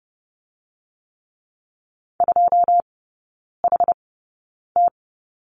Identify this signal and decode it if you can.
Morse code, heard as a keyed tone, about 30 wpm, spaced as for 9 wpm, 690 Hz. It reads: 2HT
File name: Lesson 16e - Letter c - Three Characters 30wpm